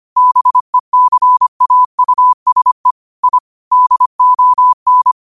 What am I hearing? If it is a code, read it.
Morse code, heard as a keyed tone, about 25 wpm, 1000 Hz. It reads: BECAUSE I DON
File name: qotdfiles / 25wpm